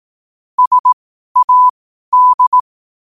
Morse code, heard as a keyed tone, about 18 wpm, spaced as for 12 wpm, 1000 Hz. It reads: SAD
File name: qotdfiles / 12wpm